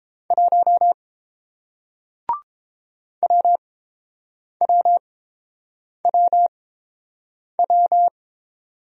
Morse code, heard as a keyed tone, about 30 wpm, spaced as for 7 wpm, 700 Hz. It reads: 1 WWWW